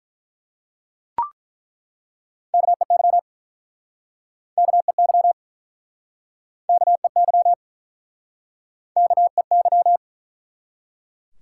Morse code, heard as a keyed tone, about 49 wpm, 700 Hz. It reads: KEY KEY KEY KEY